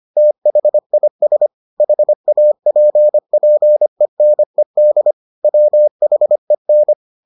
Morse code, heard as a keyed tone, about 25 wpm, 600 Hz. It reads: THIS HAPPENED WHEN